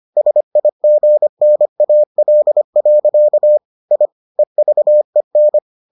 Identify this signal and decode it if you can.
Morse code, heard as a keyed tone, about 25 wpm, 600 Hz. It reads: SIGNAL. I EVEN